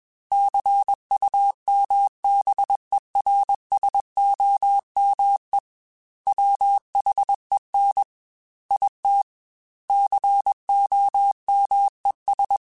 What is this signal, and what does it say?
Morse code, heard as a keyed tone, about 21 wpm, 800 Hz. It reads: CUMBERSOME WHEN IT COMES
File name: Technical Subjects / Technical Subjects-mp3s / AND Gate @20